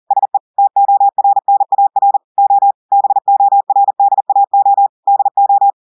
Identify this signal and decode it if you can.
Morse code, heard as a keyed tone, about 40 wpm, 800 Hz. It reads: SE TORNAR O BORDAO DO
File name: audios / portugais / portugais texte en clair 2 vitesse 40 mots minutes